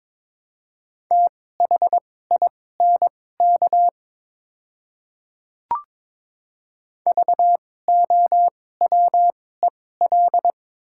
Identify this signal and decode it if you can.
Morse code, heard as a keyed tone, about 21 wpm, spaced as for 16 wpm, 700 Hz. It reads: THINK VOWEL
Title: 2x Character Spacing - Top 500 Words - 22wpm